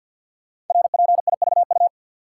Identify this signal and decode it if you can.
Morse code, heard as a keyed tone, about 50 wpm, 690 Hz. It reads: RPIVU
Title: Code Groups - 5 Letters - 50wpm